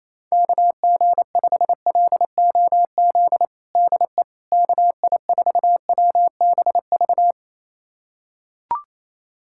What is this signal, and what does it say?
Morse code, heard as a keyed tone, about 28 wpm, 700 Hz. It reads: KG5LOZ DE KI4WBV
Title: USA Callsign DE Callsign - 6 Characters - 28wpm